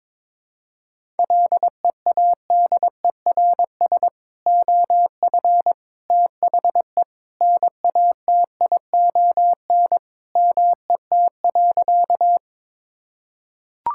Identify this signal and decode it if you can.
Morse code, heard as a keyed tone, about 21 wpm, 700 Hz. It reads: LEADERS OF THE NATION MET.
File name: practice / 22wpm / Sentences with 3-Word Phrases - 22wpm